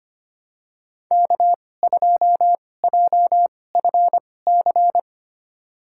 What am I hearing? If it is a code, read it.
Morse code, heard as a keyed tone, about 25 wpm, spaced as for 18 wpm, 700 Hz. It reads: K2JFC
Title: USA Call Signs - 5 Characters - 2x Character Spacing - 25wpm